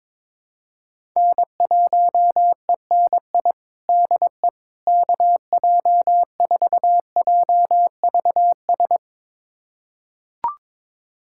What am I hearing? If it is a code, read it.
Morse code, heard as a keyed tone, about 22 wpm, 700 Hz. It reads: N1ENI DE KJ4JVS
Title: USA Callsign DE Callsign 22wpm